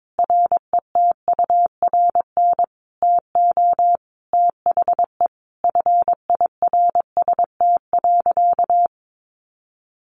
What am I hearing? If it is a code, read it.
Morse code, heard as a keyed tone, about 22 wpm, 700 Hz. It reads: RETURN TO THE FIRST.
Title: Sentences from Top 800 Words 22wpm